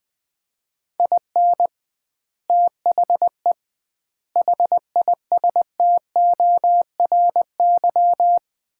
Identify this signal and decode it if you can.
Morse code, heard as a keyed tone, about 21 wpm, 700 Hz. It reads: IN THE HISTORY
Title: 2x Word-Spacing - Sets of 5 Words - 20wpm